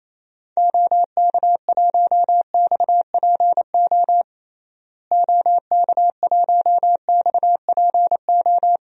Morse code, heard as a keyed tone, about 28 wpm, 700 Hz. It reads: OK1XPO OK1XPO
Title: DXCC-100 Call Signs - 6 Characters - Repeated 2x - 28wpm